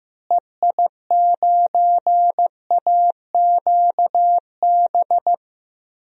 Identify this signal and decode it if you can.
Morse code, heard as a keyed tone, about 15 wpm, 700 Hz. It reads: EI9AQB
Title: DXCC-100 Call Signs - 6 Characters - Repeated 2x - 15wpm